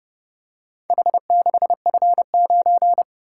Morse code, heard as a keyed tone, about 29 wpm, 700 Hz. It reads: H6F9